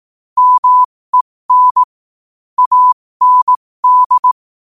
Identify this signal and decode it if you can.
Morse code, heard as a keyed tone, about 18 wpm, spaced as for 15 wpm, 1000 Hz. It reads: MEN AND
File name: qotdfiles / 15wpm